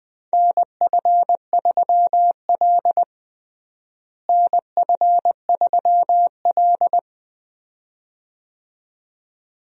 Morse code, heard as a keyed tone, about 21 wpm, 700 Hz. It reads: NF3L NF3L
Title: USA Call Signs - 4 Characters - Repeated 2x - 20wpm